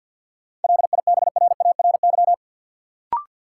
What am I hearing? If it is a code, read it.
Morse code, heard as a keyed tone, about 52 wpm, 700 Hz. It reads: LIBRARY
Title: Sentences from Top 600 Words - New Word Drill - 50wpm